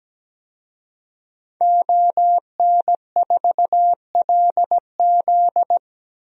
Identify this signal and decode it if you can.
Morse code, heard as a keyed tone, about 17 wpm, 700 Hz. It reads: ON4LZ
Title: DXCC-100 Call Signs - 5 Characters - Repeated 3x - 17wpm